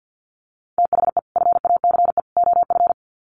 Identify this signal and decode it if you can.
Morse code, heard as a keyed tone, about 50 wpm, 700 Hz. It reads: THE FACE OF
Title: Sets of 3 Words 50wpm